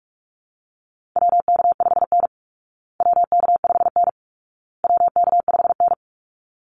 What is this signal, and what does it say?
Morse code, heard as a keyed tone, about 44 wpm, 700 Hz. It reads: WK5N WK5N WK5N
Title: Call Signs Repeated 2x - 45wpm